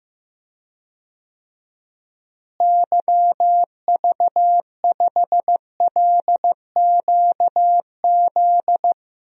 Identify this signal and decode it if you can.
Morse code, heard as a keyed tone, about 15 wpm, 700 Hz. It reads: YV5LQZ
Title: DXCC-100 Call Signs - 6 Characters - Repeated 2x - 15wpm